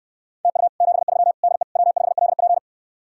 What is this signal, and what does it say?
Morse code, heard as a keyed tone, about 66 wpm, 700 Hz. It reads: TU 73 DE W5LP